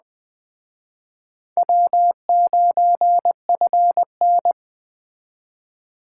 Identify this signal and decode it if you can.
Morse code, heard as a keyed tone, about 21 wpm, 700 Hz. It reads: W9FN
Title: Speed Racing - USA Call Signs 17wpm